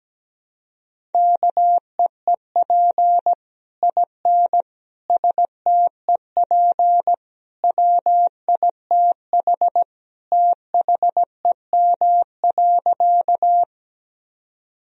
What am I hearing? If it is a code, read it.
Morse code, heard as a keyed tone, about 17 wpm, 700 Hz. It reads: KEEP IN STEP WITH THEM.
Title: Sentences from Top 1000 Words - Review Part 2 - 17wpm